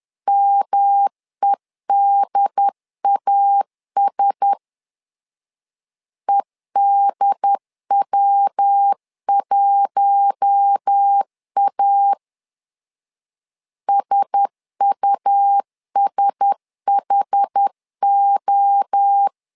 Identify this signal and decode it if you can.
Morse code, heard as a keyed tone, about 10 wpm, 800 Hz. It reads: MEDAS EDW1A SUSHO
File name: LEZIONI / 4 Q1S 10wpm 2133